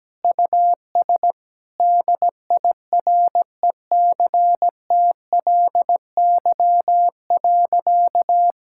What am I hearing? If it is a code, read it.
Morse code, heard as a keyed tone, about 17 wpm, 700 Hz. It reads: US DIRECTLY.
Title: Sentences from Top 1000 Words - Review Part 5 - 17wpm